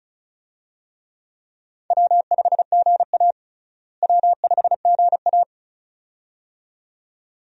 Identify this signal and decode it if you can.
Morse code, heard as a keyed tone, about 36 wpm, 700 Hz. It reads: W5GA W5GA
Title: USA Call Signs - 4 Characters - Repeated 2x - 35wpm